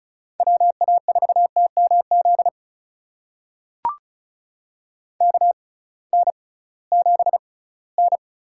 Morse code, heard as a keyed tone, about 36 wpm, spaced as for 20 wpm, 700 Hz. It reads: WA4TMZ KN7N
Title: USA Call Signs - 6x Character Spacing - 35wpm